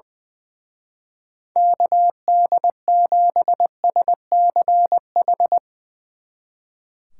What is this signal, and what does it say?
Morse code, heard as a keyed tone, about 22 wpm, 700 Hz. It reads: KD7SCH
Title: Speed Racing - USA 2x3 Call Signs 20wpm